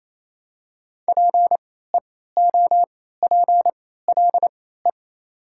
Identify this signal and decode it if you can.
Morse code, heard as a keyed tone, about 28 wpm, spaced as for 16 wpm, 700 Hz. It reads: PEOPLE
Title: 3x Character Spacing - Top 100 Words - 28wpm